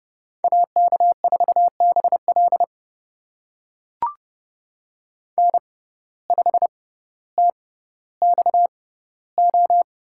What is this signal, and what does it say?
Morse code, heard as a keyed tone, about 30 wpm, spaced as for 15 wpm, 700 Hz. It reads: AK4BL N5TXO
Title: USA Call Signs - 5 Characters - 6x Character Spacing - 30wpm